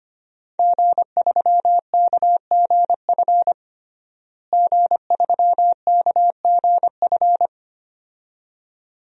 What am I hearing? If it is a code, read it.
Morse code, heard as a keyed tone, about 25 wpm, 700 Hz. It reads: G3KGF G3KGF
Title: DXCC-100 Call Signs - 5 Characters - Repeated 2x - 25wpm